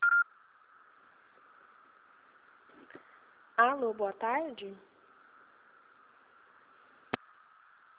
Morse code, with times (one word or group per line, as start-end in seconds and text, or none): none